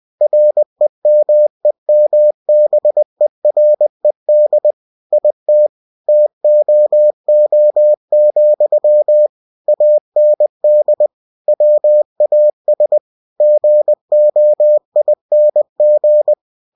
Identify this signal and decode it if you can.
Morse code, heard as a keyed tone, about 20 wpm, 600 Hz. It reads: REMEMBERED IT TOO, AND WAS GOING
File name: cw-books / winnie-the-pooh / winnie-20wpm-page-0037